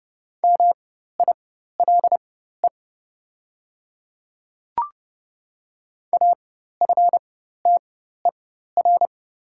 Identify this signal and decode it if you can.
Morse code, heard as a keyed tone, about 30 wpm, spaced as for 14 wpm, 700 Hz. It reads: MILE AFTER
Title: 4x Character Spacing - Top 500 Words - 30wpm